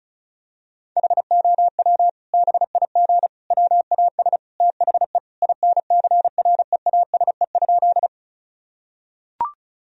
Morse code, heard as a keyed tone, about 36 wpm, 700 Hz. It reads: HOW BIG WAS THE INCREASE?
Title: Sentences from Top 1000 Words - Review Part 4 - 35wpm